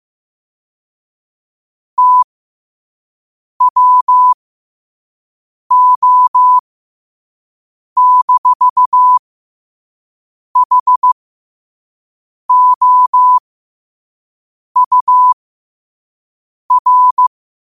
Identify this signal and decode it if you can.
Morse code, heard as a keyed tone, about 15 wpm, spaced as for 5 wpm, 1000 Hz. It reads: TWO-HOUR
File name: qotdfiles / 5wpm